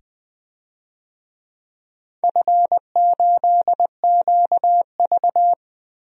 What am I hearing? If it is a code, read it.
Morse code, heard as a keyed tone, about 20 wpm, 700 Hz. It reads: F8QV
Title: DXCC-100 Call Signs - 4 Characters - 20wpm